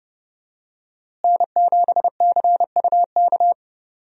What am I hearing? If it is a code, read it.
Morse code, heard as a keyed tone, about 32 wpm, 700 Hz. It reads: N7CUK
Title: USA Call Signs - 5 Characters - Repeated 2x - 30wpm